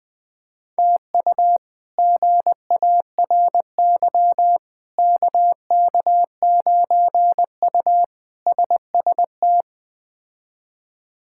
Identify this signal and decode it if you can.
Morse code, heard as a keyed tone, about 20 wpm, 700 Hz. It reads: TU GARY KK9U SST
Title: K1USN SST QSO Practice - 20wpm